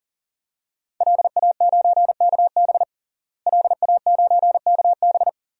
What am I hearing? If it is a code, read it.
Morse code, heard as a keyed tone, about 40 wpm, 700 Hz. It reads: LA9KB LA9KB
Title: DXCC-100 Call Signs - 5 Characters - Repeated 3x - 40wpm